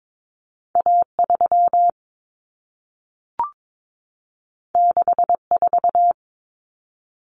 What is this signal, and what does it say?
Morse code, heard as a keyed tone, about 22 wpm, 700 Hz. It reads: A3 64
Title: Common Cut Numbers - 2-Digit 22wpm